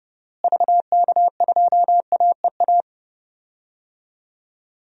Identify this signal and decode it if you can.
Morse code, heard as a keyed tone, about 29 wpm, 700 Hz. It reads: VK2AEA
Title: DXCC-100 Call Signs - 6 Characters - Repeated 2x - 30wpm